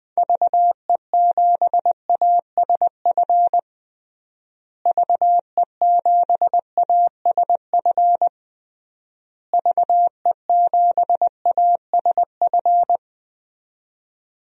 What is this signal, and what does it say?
Morse code, heard as a keyed tone, about 20 wpm, 700 Hz. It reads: VE7ASF VE7ASF VE7ASF
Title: DXCC-100 Call Signs - 6 Characters - Repeated 3x - 20wpm